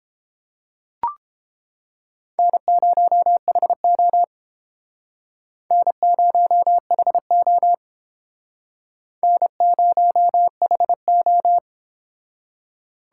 Morse code, heard as a keyed tone, about 29 wpm, 700 Hz. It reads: N0HO N0HO N0HO